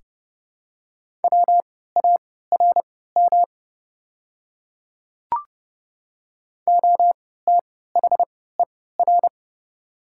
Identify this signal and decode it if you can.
Morse code, heard as a keyed tone, about 30 wpm, spaced as for 17 wpm, 690 Hz. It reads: WARM OTHER
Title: 3x Character Spacing - Top 500 Words - 30wpm